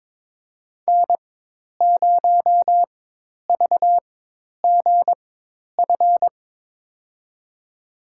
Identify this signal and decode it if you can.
Morse code, heard as a keyed tone, about 22 wpm, spaced as for 10 wpm, 700 Hz. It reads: N0VGF
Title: USA Call Signs - 5 Characters - 4x Character Spacing - 22wpm